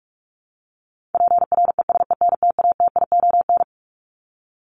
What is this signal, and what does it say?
Morse code, heard as a keyed tone, about 44 wpm, 700 Hz. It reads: PRESENTATION
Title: Top 1501-1750 Words 45wpm